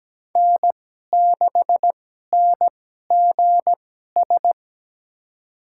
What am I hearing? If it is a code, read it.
Morse code, heard as a keyed tone, about 17 wpm, spaced as for 12 wpm, 700 Hz. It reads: N6NGS
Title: USA Call Signs - 5 Characters - 2x Character Spacing - 17wpm